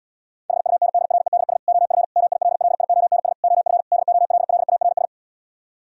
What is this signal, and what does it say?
Morse code, heard as a keyed tone, about 78 wpm, 700 Hz. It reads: HUNDREDS OF MILLIONS OF DOLLARS